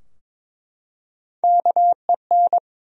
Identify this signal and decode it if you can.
Morse code, heard as a keyed tone, about 22 wpm, 700 Hz. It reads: KEN
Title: USA Common Names - Max 3 Letters - 22wpm